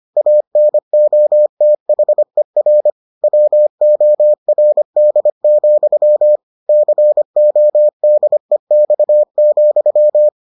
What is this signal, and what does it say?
Morse code, heard as a keyed tone, about 25 wpm, 600 Hz. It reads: ANOTHER WORD, CODEX,